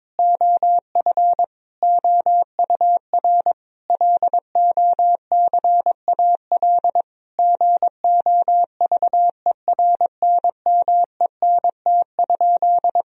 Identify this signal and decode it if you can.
Morse code, heard as a keyed tone, about 22 wpm, 700 Hz. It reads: OF OUR LOCAL GOVERNMENT?
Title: Sentences from Top 1000 Words - Review Part 4 - 22wpm